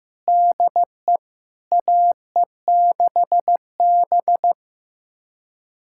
Morse code, heard as a keyed tone, about 15 wpm, 700 Hz. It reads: DE AE6B
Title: USA Callsign DE Callsign - 4 Characters - 15wpm